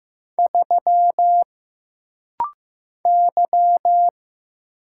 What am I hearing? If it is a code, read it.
Morse code, heard as a keyed tone, about 15 wpm, 700 Hz. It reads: 3 Y
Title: Single Letter-Number Alt - Rapid-Fire 15wpm